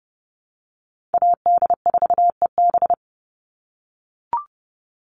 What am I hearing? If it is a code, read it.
Morse code, heard as a keyed tone, about 30 wpm, 700 Hz. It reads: AD4EB
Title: Call Signs - 30wpm